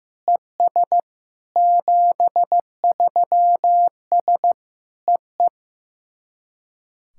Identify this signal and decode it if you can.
Morse code, heard as a keyed tone, about 15 wpm, 700 Hz. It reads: ES 73S EE
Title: SOTA QSO Practice 15wpm